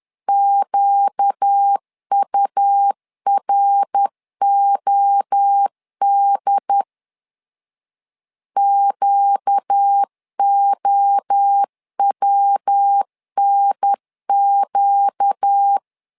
Morse code, heard as a keyed tone, about 10 wpm, 800 Hz. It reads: QUROD QOWNQ
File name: LEZIONI / 4 Q 10wpm 2210